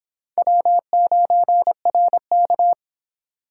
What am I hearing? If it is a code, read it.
Morse code, heard as a keyed tone, about 25 wpm, 700 Hz. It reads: W9RK